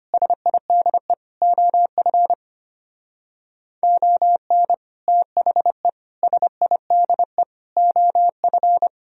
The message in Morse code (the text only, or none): SIDE OF ON THE SIDE OF